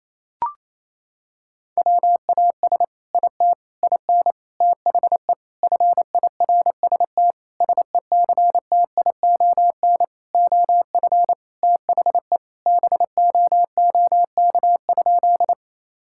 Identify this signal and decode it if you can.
Morse code, heard as a keyed tone, about 28 wpm, 700 Hz. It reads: WAS IT IN THE FIRST SECTION OF THE BOOK?
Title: Sentences from Top 1000 Words - Review Part 5 - 28wpm